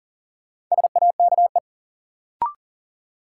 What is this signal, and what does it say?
Morse code, heard as a keyed tone, about 40 wpm, 700 Hz. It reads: SAKE
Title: Sets of 4 Words - New Word Drill - 40wpm